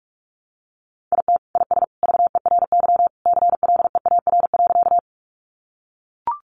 Morse code, heard as a keyed tone, about 44 wpm, 700 Hz. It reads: IT IS VERY CLEAR.
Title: Sentences from Top 500 Words 45wpm